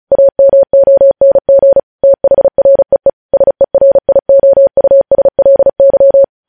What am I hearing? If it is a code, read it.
Morse code, heard as a keyed tone, about 36 wpm, 550 Hz. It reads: AMONG THREE SERIOUSLY
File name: podcast / News headlines-1x-35x35